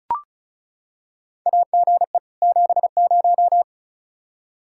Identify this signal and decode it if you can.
Morse code, heard as a keyed tone, about 36 wpm, 700 Hz. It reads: AGE 70
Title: Practice QSO - Exchange 3 - More Info - 35wpm